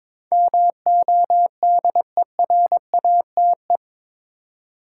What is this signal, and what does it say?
Morse code, heard as a keyed tone, about 22 wpm, 700 Hz. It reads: MODERATE
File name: practice / 22wpm / Eight Letter Words - 22wpm